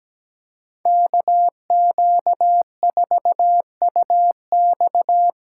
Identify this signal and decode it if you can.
Morse code, heard as a keyed tone, about 20 wpm, 700 Hz. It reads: KQ4UX
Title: Speed Racing - USA 2x2 Call Signs 17wpm